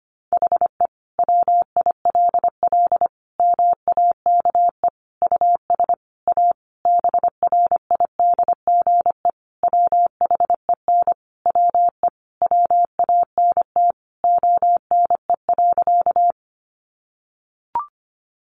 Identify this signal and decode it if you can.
Morse code, heard as a keyed tone, about 25 wpm, 700 Hz. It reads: HE WILL MAKE US A BRIDGE WHEN WE WANT ONE.
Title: Sentences from Top 1200 Words 25wpm